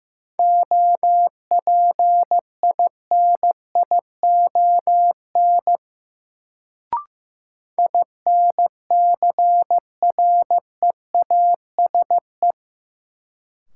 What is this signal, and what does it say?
Morse code, heard as a keyed tone, about 15 wpm, 700 Hz. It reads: OPINION INCREASE
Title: Sentences from Top 1200 Words - New Word Drill Alt - 15wpm